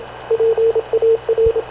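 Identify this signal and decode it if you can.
Morse code, heard as a keyed tone, about 27 wpm, 450 Hz, with radio noise. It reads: PAR
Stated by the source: Havana Radio CLA press (PX) broadcast, recorded by Radio Officer Paul Dumesnil aboard mv CNAV QUEST/CZDO.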